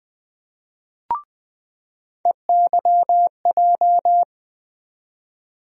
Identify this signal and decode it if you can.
Morse code, heard as a keyed tone, about 20 wpm, 700 Hz. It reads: EYJ
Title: USA Call Sign Postfixes - 3 Letters 20wpm